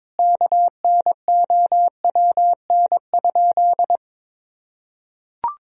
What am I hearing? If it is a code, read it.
Morse code, heard as a keyed tone, about 22 wpm, 700 Hz. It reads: KNOWN?
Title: Sentences from Top 1000 Words - Review Part 3 - 22wpm